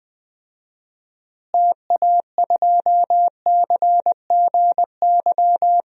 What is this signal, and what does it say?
Morse code, heard as a keyed tone, about 20 wpm, 700 Hz. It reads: TA2CGY
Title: DXCC-100 Call Signs - 6 Characters - Repeated 2x - 20wpm